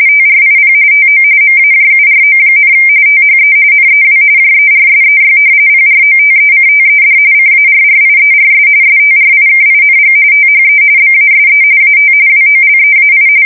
RTTY, about 45.45 baud. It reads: TRIP TO MAJURO, THE NEW JERSEY PROTECTED CARRIERS STRIKING THE BASE AT WOLEAI
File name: fsk170 / battleshipnj